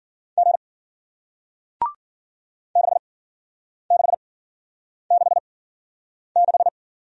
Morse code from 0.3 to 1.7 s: G

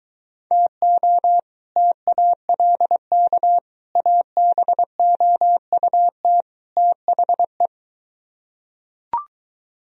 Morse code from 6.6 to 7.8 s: THE

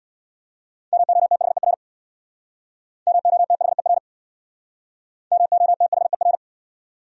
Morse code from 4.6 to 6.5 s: MOTHER

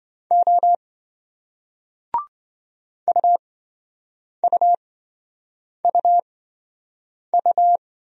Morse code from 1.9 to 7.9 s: UUUU

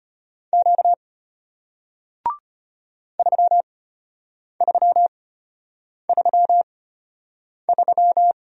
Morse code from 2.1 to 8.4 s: 3333

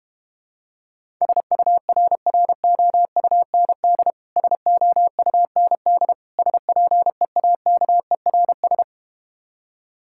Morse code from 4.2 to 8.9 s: SOUND SPEAKERS

